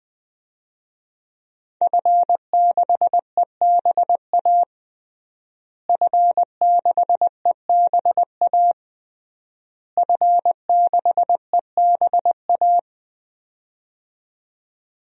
Morse code, heard as a keyed tone, about 20 wpm, 700 Hz. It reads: F6EBA F6EBA F6EBA
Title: DXCC-100 Call Signs - 5 Characters - Repeated 3x - 20wpm